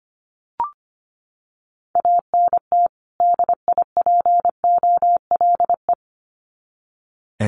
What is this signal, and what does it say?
Morse code, heard as a keyed tone, about 25 wpm, 700 Hz. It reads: ANT DIPOLE
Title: Common QSO Phrases 25wpm